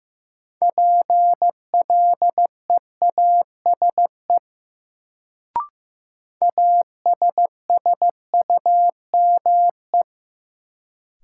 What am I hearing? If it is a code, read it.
Morse code, heard as a keyed tone, about 15 wpm, 700 Hz. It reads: PLEASE ASSUME
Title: Sentences from Top 1500 Words - New Word Drill - 15wpm